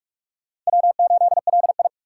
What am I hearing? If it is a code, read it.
Morse code, heard as a keyed tone, about 41 wpm, 700 Hz. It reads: W8LI
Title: USA Call Signs - 4 Characters - 3x Character Spacing - 45wpm